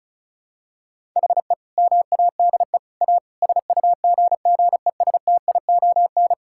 Morse code, heard as a keyed tone, about 36 wpm, 700 Hz. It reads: HE MADE A SUGGESTION
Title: Sentences from Top 1600 Words 35wpm